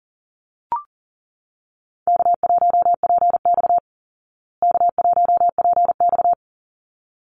Morse code, heard as a keyed tone, about 40 wpm, 700 Hz. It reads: K1PX K1PX
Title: Easy Call Signs Repeated 2x - 40wpm